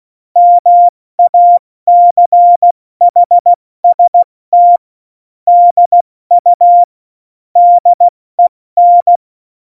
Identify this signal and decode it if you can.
Morse code, heard as a keyed tone, about 16 wpm, 700 Hz. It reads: MACHST DU DEN
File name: Morsekurs / Witze / Witz 17. Oktober 17, Tempo 70